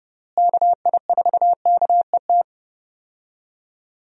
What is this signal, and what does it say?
Morse code, heard as a keyed tone, about 30 wpm, 700 Hz. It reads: KI4KET
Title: USA Call Signs - 6 Characters - Repeated 3x - 30wpm